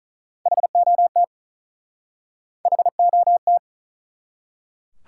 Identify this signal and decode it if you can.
Morse code, heard as a keyed tone, about 48 wpm, 700 Hz. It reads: HOT HOT